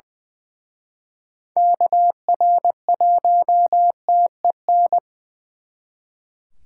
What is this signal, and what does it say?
Morse code, heard as a keyed tone, about 25 wpm, 700 Hz. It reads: KR1TEN
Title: Speed Racing - USA 2x3 Call Signs 20wpm